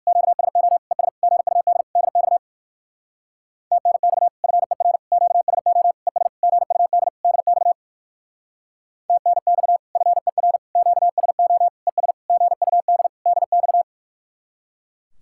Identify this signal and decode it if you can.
Morse code, heard as a keyed tone, about 59 wpm, 700 Hz. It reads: QSO ES GUD DX TNX FER QSO ES GUD DX TNX FER QSO ES GUD DX